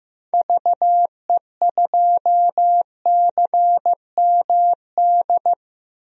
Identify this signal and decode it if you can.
Morse code, heard as a keyed tone, about 15 wpm, 700 Hz. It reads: VE2CMD
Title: DXCC-100 Call Signs - 6 Characters - Repeated 3x - 15wpm